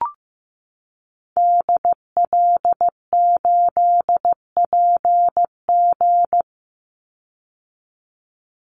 Morse code, heard as a keyed tone, about 15 wpm, 700 Hz. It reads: DL8PG
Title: Call Signs Repeated 2x - 15wpm